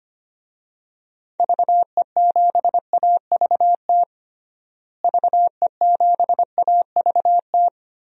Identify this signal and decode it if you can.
Morse code, heard as a keyed tone, about 25 wpm, 700 Hz. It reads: VE7AVT VE7AVT
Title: DXCC-100 Call Signs - 6 Characters - Repeated 3x - 25wpm